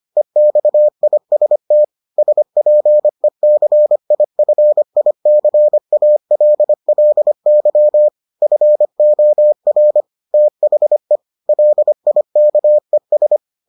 Morse code, heard as a keyed tone, about 25 wpm, 600 Hz. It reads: EXIST SPECIFICALLY FOR THE LIKES